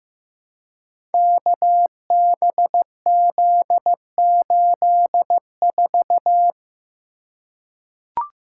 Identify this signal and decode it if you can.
Morse code, heard as a keyed tone, about 15 wpm, 700 Hz. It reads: KBZ84
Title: Code Groups - 5 Letter:Numbers - 15wpm